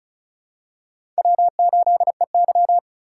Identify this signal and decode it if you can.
Morse code, heard as a keyed tone, about 36 wpm, 700 Hz. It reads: W8EY
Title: USA Call Signs - 4x Character Spacing - 35wpm